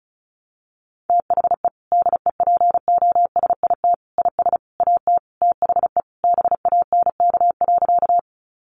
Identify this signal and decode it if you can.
Morse code, heard as a keyed tone, about 34 wpm, 700 Hz. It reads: THE DEPOSIT IS AT THE BANK.
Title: Sentences from Top 1300 Words 35wpm